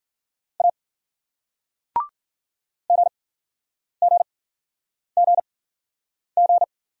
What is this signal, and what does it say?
Morse code, heard as a keyed tone, about 52 wpm, spaced as for 9 wpm, 700 Hz. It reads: A GGGG